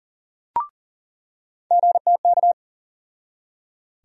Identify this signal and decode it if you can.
Morse code, heard as a keyed tone, about 40 wpm, 700 Hz. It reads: GTK
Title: USA Call Sign Postfixes - 3 Letters 40wpm